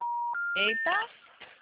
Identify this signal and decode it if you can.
Special information tone, with speech.